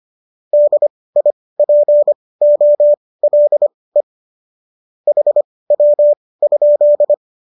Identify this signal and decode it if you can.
Morse code, heard as a keyed tone, about 25 wpm, spaced as for 18 wpm, 590 Hz. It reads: DIPOLE HW?